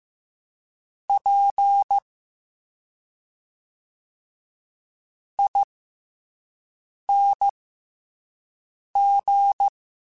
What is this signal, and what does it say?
Morse code, heard as a keyed tone, about 15 wpm, spaced as for 5 wpm, 800 Hz. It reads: P ING